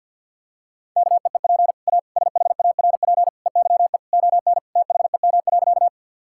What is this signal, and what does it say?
Morse code, heard as a keyed tone, about 50 wpm, 700 Hz. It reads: KEEP A SHARP EYE ON THEM.
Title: Sentences from Top 1600 Words 50wpm